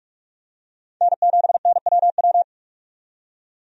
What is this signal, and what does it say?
Morse code, heard as a keyed tone, about 46 wpm, 700 Hz. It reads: N7NWW
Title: USA Call Signs - 5 Characters - 45wpm